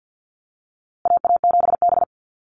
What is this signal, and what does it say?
Morse code, heard as a keyed tone, about 50 wpm, 700 Hz. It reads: AA7B
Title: USA Call Signs Repeated 2x 50wpm